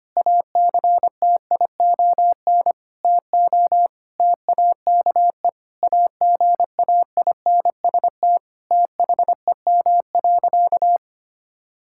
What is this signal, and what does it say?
Morse code, heard as a keyed tone, about 25 wpm, 700 Hz. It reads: ACTION TO TAKE AGAINST THEM.
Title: Sentences from Top 1000 Words - Review Part 6 - 25wpm